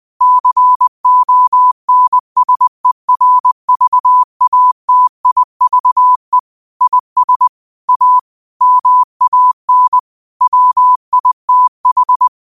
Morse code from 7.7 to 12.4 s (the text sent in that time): A MAN WITH